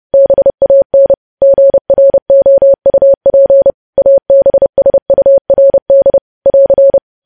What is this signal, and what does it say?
Morse code, heard as a keyed tone, about 30 wpm, 550 Hz. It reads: BAN GROUP ABSURD <AR>
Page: news headlines in morse code at 30 wpm